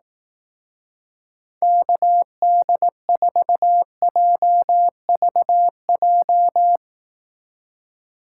Morse code, heard as a keyed tone, about 18 wpm, 700 Hz. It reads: KD4JVJ
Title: Speed Racing - USA 2x3 Call Signs 15wpm